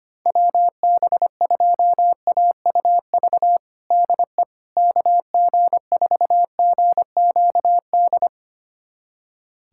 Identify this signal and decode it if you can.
Morse code, heard as a keyed tone, about 25 wpm, 700 Hz. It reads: WB2AUV DE KG4GQD
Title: USA Callsign DE Callsign - 6 Characters - 25wpm